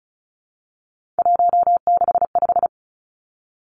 Morse code, heard as a keyed tone, about 36 wpm, 700 Hz. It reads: 165
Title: Three Challenging Letter-Number Combinations 35wpm